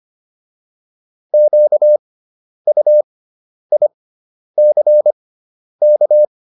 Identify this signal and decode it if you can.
Morse code, heard as a keyed tone, about 25 wpm, spaced as for 10 wpm, 600 Hz. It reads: QUICK